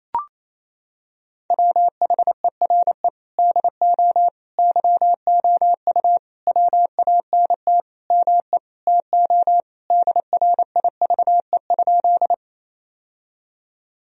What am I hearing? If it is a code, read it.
Morse code, heard as a keyed tone, about 28 wpm, 700 Hz. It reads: WHERE DO YOU WANT ME TO DRIVE?
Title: Sentences from Top 1000 Words - Review Part 2 - 28wpm